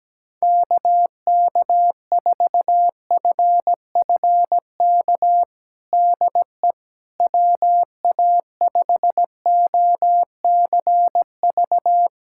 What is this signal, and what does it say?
Morse code, heard as a keyed tone, about 17 wpm, 700 Hz. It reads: KK4FFK DE WA5OCV
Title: USA Callsign DE Callsign - 6 Characters - 17wpm